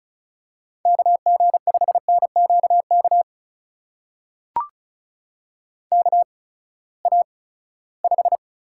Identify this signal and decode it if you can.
Morse code, heard as a keyed tone, about 36 wpm, spaced as for 20 wpm, 700 Hz. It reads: KG5NQK KA5
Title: USA Call Signs - 6 Characters - 8x Character Spacing - 35wpm